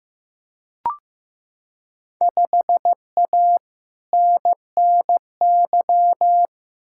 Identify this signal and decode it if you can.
Morse code, heard as a keyed tone, about 15 wpm, 700 Hz. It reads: 5A NNY
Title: Field Day Exchanges - 15wpm